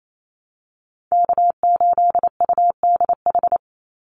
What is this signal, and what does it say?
Morse code, heard as a keyed tone, about 28 wpm, 700 Hz. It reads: K8UDH
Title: Call Signs - 28wpm